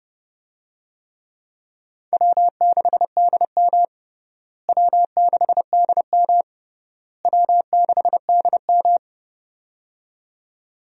Morse code, heard as a keyed tone, about 31 wpm, 700 Hz. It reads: W6DM W6DM W6DM
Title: USA Call Signs - 4 Characters - Repeated 3x - 30wpm